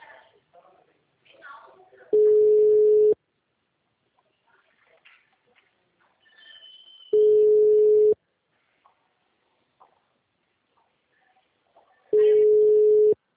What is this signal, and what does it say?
Ringback tone (Europe, 425 Hz), with speech.
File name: out- -2019- - - .16640